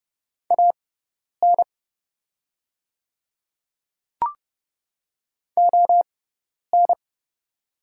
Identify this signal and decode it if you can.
Morse code, heard as a keyed tone, about 30 wpm, spaced as for 10 wpm, 700 Hz. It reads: AN ON
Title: 6x Character Spacing - Top 200 Words - 30wpm